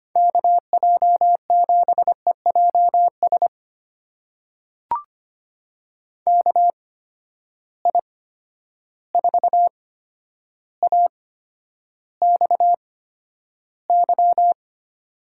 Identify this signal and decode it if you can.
Morse code, heard as a keyed tone, about 25 wpm, spaced as for 11 wpm, 700 Hz. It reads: KJ7EJS KI4AXY
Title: USA Call Signs - 8x Character Spacing - 25wpm